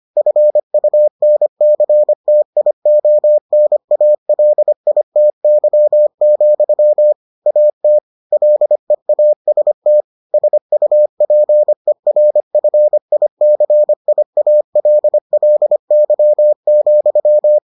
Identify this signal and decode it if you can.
Morse code, heard as a keyed tone, about 25 wpm, 600 Hz. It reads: FUNCTIONALITY, AT LEAST SUPERFICIALLY,